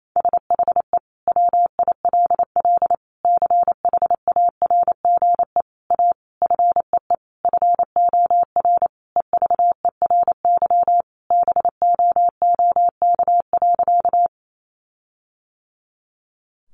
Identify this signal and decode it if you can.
Morse code, heard as a keyed tone, about 28 wpm, 700 Hz. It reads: SHE WILL CHARGE A FEE FOR EVERY BOOK.
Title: Sentences from Top 1500 Words 28wpm